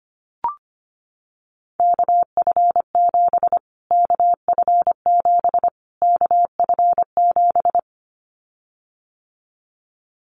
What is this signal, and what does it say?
Morse code, heard as a keyed tone, about 25 wpm, 700 Hz. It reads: KF7 KF7 KF7
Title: Callsign Prefixes for Top 25 DXCC - Repeated 3x - 25wpm